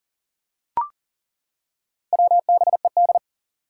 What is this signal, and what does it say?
Morse code, heard as a keyed tone, about 38 wpm, 700 Hz. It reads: WBED
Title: Four Challenging Letter-Number Combinations 40wpm